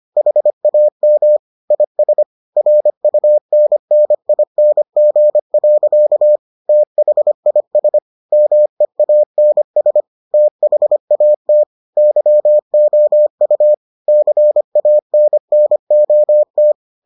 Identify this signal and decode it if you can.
Morse code, heard as a keyed tone, about 25 wpm, 600 Hz. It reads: HAM IS RUNNING. THIS MEANS THAT YOU CANNOT